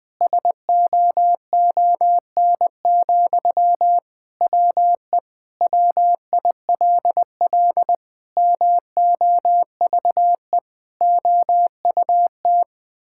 Morse code, heard as a keyed tone, about 20 wpm, 700 Hz. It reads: SOON, WE WILL MOVE OUT
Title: Sentences from Top 1000 Words - Review Part 6 - 20wpm